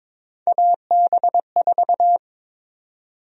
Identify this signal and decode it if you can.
Morse code, heard as a keyed tone, about 22 wpm, 700 Hz. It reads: AB4
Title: USA Call Sign Prefixes - 22wpm